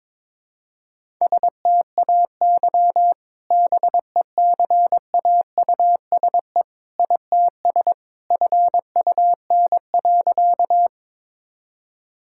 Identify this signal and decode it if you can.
Morse code, heard as a keyed tone, about 22 wpm, 700 Hz. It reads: STAY BECAUSE ITS FUN.
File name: practice / 22wpm / Sentences with 2-Word Phrases - 22wpm